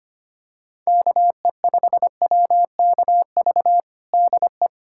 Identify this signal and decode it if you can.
Morse code, heard as a keyed tone, about 25 wpm, 700 Hz. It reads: KE5WKV DE
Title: USA Callsign DE Callsign - 6 Characters - 25wpm